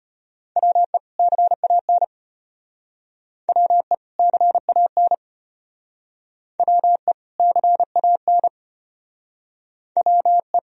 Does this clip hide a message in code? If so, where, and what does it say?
Yes: Morse code from 0.6 to 10.6 s, heard as a keyed tone, about 33 wpm, 700 Hz. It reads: WE CAN WE CAN WE CAN WE